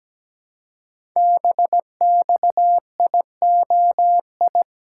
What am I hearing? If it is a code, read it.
Morse code, heard as a keyed tone, about 17 wpm, 700 Hz. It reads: BXIOI
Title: Code Groups - 5 Letters - 17wpm